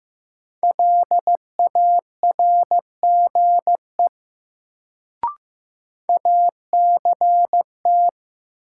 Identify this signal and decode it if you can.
Morse code, heard as a keyed tone, about 15 wpm, 700 Hz. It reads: LARGE ACT
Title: Sentences from Top 200 Words - New Word Drill - 15wpm